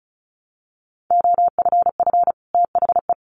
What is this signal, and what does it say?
Morse code, heard as a keyed tone, about 34 wpm, 700 Hz. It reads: OFF THE
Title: Sets of 2 Words 35wpm